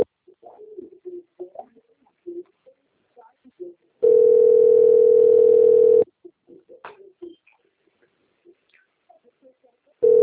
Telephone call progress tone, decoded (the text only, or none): ringback tone